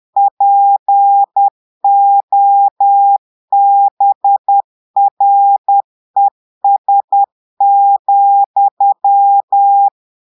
Morse code, heard as a keyed tone, about 10 wpm, 800 Hz. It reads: POBRES,
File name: Portugais - texte en clair 1 - vitesse - 10 mots minute